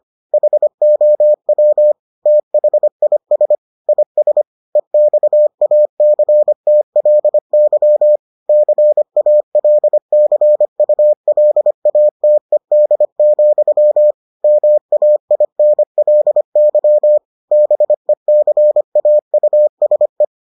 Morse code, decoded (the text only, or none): HOW THIS IS EXACTLY CALCULATED, MAINLY BECAUSE